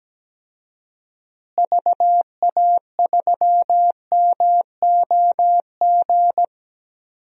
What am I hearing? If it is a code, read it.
Morse code, heard as a keyed tone, about 17 wpm, 700 Hz. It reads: VA3MOG
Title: DXCC-100 Call Signs - 6 Characters - Repeated 2x - 17wpm